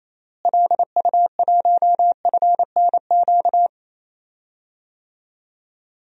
Morse code, heard as a keyed tone, about 28 wpm, 700 Hz. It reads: LU1FNQ
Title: DXCC-100 Call Signs - 6 Characters - Repeated 2x - 28wpm